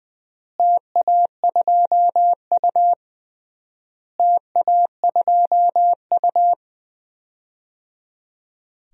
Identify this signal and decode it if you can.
Morse code, heard as a keyed tone, about 21 wpm, 700 Hz. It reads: TA2U TA2U
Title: DXCC-100 Call Signs - 4 Characters - Repeated 3x - 20wpm